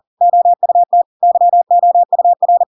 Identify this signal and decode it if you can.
Morse code, heard as a keyed tone, about 40 wpm, 700 Hz. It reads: OUT YOUR
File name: Pinocchio / Pinocchio ch6 40 40 ik1hge